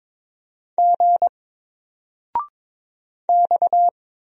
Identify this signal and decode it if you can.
Morse code, heard as a keyed tone, about 22 wpm, 700 Hz. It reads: G X
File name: Single-Letter-Number-22wpm